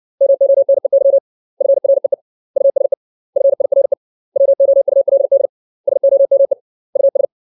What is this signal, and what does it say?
Morse code, heard as a keyed tone, about 60 wpm, 550 Hz. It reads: MONEY FREE USE FIRE WOULD SOME US
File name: 632 words / 632 Words 60WPM